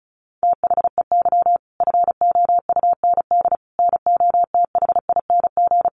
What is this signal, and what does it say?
Morse code, heard as a keyed tone, about 34 wpm, 700 Hz. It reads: THEY FOUND NOTHING
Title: Sentences from Top 1500 Words 35wpm